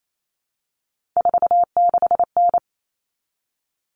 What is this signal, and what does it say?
Morse code, heard as a keyed tone, about 28 wpm, 700 Hz. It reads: 46N
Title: Common Cut Numbers - 3-Digit 28wpm